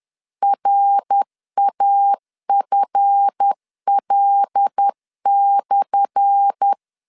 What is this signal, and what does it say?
Morse code, heard as a keyed tone, about 10 wpm, 800 Hz. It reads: RAFL/
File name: LEZIONI / 10 SIGNS / 10A SLASH 10wpm 2808